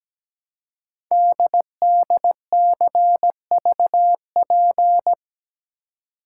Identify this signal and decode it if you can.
Morse code, heard as a keyed tone, about 17 wpm, 700 Hz. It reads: DDCVP
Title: Code Groups - 5 Letters - 17wpm